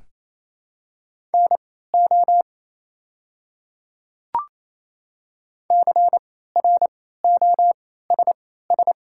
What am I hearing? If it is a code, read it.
Morse code, heard as a keyed tone, about 27 wpm, spaced as for 16 wpm, 700 Hz. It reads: NO CROSS
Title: 3x Character Spacing - Top 300 Words - 28wpm